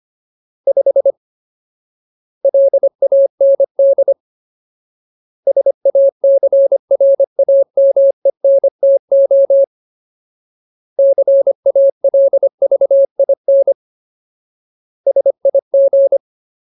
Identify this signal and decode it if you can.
Morse code, heard as a keyed tone, about 25 wpm, 550 Hz. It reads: 5 LAND SACRAMENTO CALVIN SIG
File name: Ultimate Word List / 25WPM Ultimate